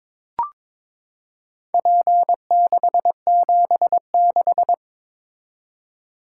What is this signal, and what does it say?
Morse code, heard as a keyed tone, about 22 wpm, 700 Hz. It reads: P676
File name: practice / 22wpm / Four Letter-Number Combinations - 22wpm